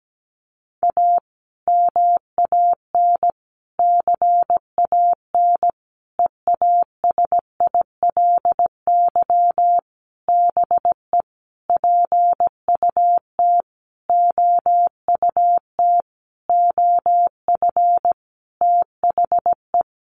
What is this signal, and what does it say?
Morse code, heard as a keyed tone, about 17 wpm, 700 Hz. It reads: A MAN CAN EASILY BE PUT OUT OF THE
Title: Sentences from Top 900 Words 17wpm